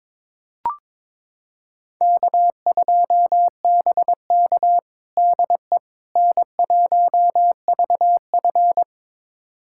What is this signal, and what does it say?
Morse code, heard as a keyed tone, about 21 wpm, 700 Hz. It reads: K2BK DE N1VF
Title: USA Callsign DE Callsign - 4 Characters - 22wpm